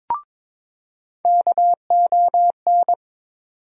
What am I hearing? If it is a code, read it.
Morse code, heard as a keyed tone, about 22 wpm, 700 Hz. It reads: KON
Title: USA Call Sign Postfixes - 3 Letters 22wpm